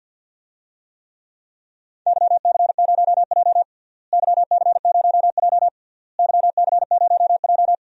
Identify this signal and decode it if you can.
Morse code, heard as a keyed tone, about 50 wpm, 700 Hz. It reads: YC0J YC0J YC0J
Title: DXCC-100 Call Signs - 4 Characters - Repeated 3x - 50wpm